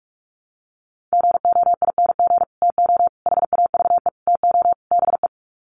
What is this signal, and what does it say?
Morse code, heard as a keyed tone, about 44 wpm, 700 Hz. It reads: GOING TO HAVE TO BE
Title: Sets of 5 Words 45wpm